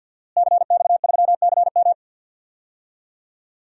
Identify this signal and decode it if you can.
Morse code, heard as a keyed tone, about 50 wpm, 700 Hz. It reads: CX3CM
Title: DXCC-100 Call Signs - 5 Characters - Repeated 3x - 50wpm